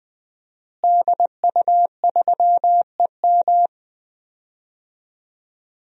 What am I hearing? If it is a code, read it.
Morse code, heard as a keyed tone, about 20 wpm, 700 Hz. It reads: DU3EM
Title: DXCC-100 Call Signs - 5 Characters - Repeated 3x - 20wpm